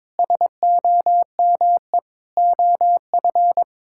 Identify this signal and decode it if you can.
Morse code, heard as a keyed tone, about 22 wpm, 700 Hz. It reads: SOME OF
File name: Sets of 9 Words - 22wpm